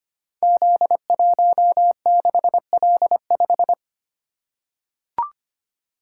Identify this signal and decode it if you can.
Morse code, heard as a keyed tone, about 25 wpm, 700 Hz. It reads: Z16L5
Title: Code Groups - 5 Letter:Numbers - 25wpm